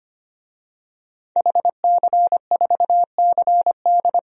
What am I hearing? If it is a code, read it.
Morse code, heard as a keyed tone, about 25 wpm, 700 Hz. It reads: HC4CD
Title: DXCC-100 Call Signs - 5 Characters - 25wpm